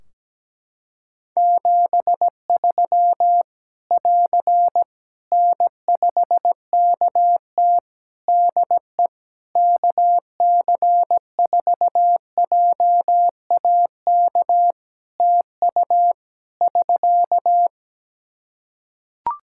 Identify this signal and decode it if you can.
Morse code, heard as a keyed tone, about 17 wpm, 700 Hz. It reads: 73 <AR> N5KT DE KC4JAK TU <SK>
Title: K1USN SST QSO Practice 17wpm